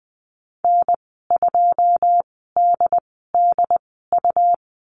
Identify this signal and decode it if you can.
Morse code, heard as a keyed tone, about 20 wpm, spaced as for 15 wpm, 700 Hz. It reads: N2DDU
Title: 167 Lesson 27g - Number 7 - Call Signs 20wpm